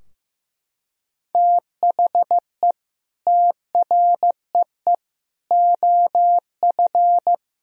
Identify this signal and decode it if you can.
Morse code, heard as a keyed tone, about 16 wpm, 700 Hz. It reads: THE TREE OF
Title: Sentences from Top 1000 Words - Review Part 3 - 15wpm